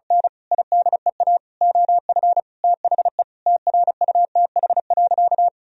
Morse code, heard as a keyed tone, about 36 wpm, 700 Hz. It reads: N IDEA OF THE TRUTH.
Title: Sentences from Top 1000 Words - Review Part 1 - 35wpm